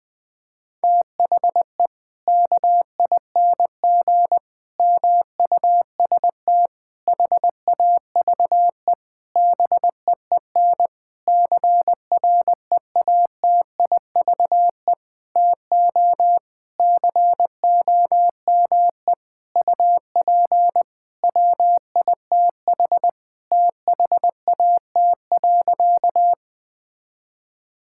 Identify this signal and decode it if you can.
Morse code, heard as a keyed tone, about 20 wpm, 700 Hz. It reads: THE KING MUST HAVE BEEN CREATIVE TO COME UP WITH THAT.
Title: Sentences from Top 1000 Words - Review Part 5 - 20wpm